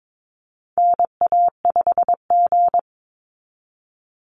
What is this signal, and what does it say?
Morse code, heard as a keyed tone, about 22 wpm, 700 Hz. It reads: NA5G
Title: Call Signs Repeated 3x - 22wpm